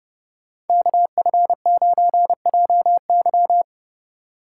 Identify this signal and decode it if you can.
Morse code, heard as a keyed tone, about 28 wpm, 700 Hz. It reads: KF9JY